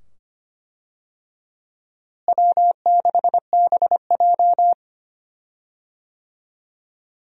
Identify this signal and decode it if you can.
Morse code, heard as a keyed tone, about 25 wpm, 700 Hz. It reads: W6BJ
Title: USA Call Signs - 4 Characters - Repeated 2x - 25wpm